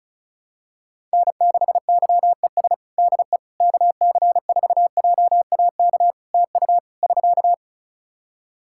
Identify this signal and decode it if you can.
Morse code, heard as a keyed tone, about 36 wpm, 700 Hz. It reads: N6YES DE KC4JAK TU <SK>
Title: K1USN SST QSO Practice 35wpm